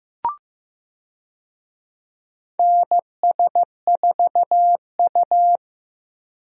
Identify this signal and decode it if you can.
Morse code, heard as a keyed tone, about 15 wpm, 700 Hz. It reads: NS4U
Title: USA Call Signs - 4 Characters - Repeated 3x - 15wpm